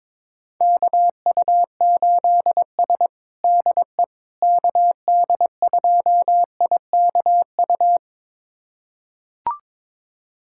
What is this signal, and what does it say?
Morse code, heard as a keyed tone, about 22 wpm, 700 Hz. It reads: KU8S DE KD2IKU
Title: USA Callsign DE Callsign 22wpm